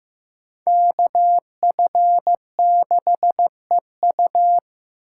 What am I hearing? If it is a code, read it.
Morse code, heard as a keyed tone, about 18 wpm, 700 Hz. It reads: KF6EU
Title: Speed Racing - USA 2x2 Call Signs 15wpm